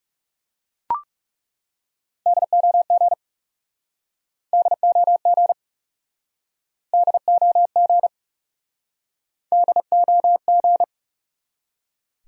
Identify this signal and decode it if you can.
Morse code, heard as a keyed tone, about 40 wpm, 700 Hz. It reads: DOG DOG DOG DOG